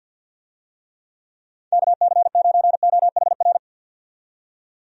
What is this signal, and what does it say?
Morse code, heard as a keyed tone, about 47 wpm, 700 Hz. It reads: KC9OHR
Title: USA Call Signs - 6 Characters - 50wpm